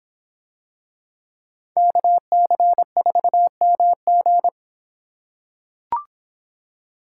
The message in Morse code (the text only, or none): KC4MG